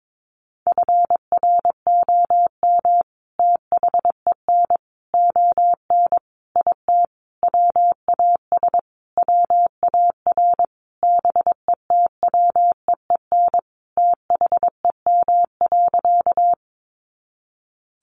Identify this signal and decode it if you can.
Morse code, heard as a keyed tone, about 22 wpm, 700 Hz. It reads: FROM THEN ON IT WAS WAR BETWEEN THEM.
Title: Sentences from Top 400 Words 22wpm